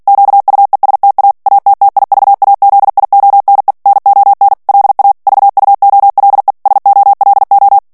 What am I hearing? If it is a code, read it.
Morse code, heard as a keyed tone, about 48 wpm, 800 Hz. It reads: QUESTA ATTIVAZIONE NON LA VUOLE SOLO
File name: morse mp3 / ITALIEN texte clair 1 vitesse 45 mots minute